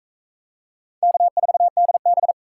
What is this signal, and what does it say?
Morse code, heard as a keyed tone, about 42 wpm, 700 Hz. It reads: K4DB